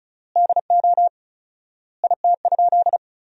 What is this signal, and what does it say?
Morse code, heard as a keyed tone, about 34 wpm, 700 Hz. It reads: DO IT?
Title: 4x Word-Spacing - Sentences from Top 200 Words - 35wpm